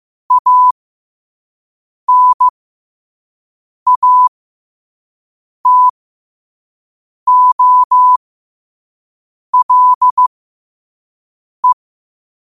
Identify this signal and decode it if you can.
Morse code, heard as a keyed tone, about 15 wpm, spaced as for 5 wpm, 1000 Hz. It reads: ANATOLE
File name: qotdfiles / 5wpm